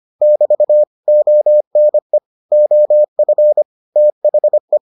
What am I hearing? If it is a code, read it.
Morse code, heard as a keyed tone, about 25 wpm, 600 Hz. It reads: <BT> ONE OF THE